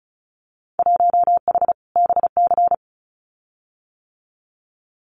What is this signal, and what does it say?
Morse code, heard as a keyed tone, about 36 wpm, 700 Hz. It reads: 1H BC
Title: Winter Field Day Exchanges Encore 35wpm